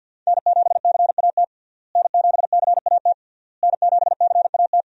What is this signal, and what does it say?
Morse code, heard as a keyed tone, about 51 wpm, 700 Hz. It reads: N7CAT N7CAT N7CAT
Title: USA Call Signs - 5 Characters - Repeated 3x - 50wpm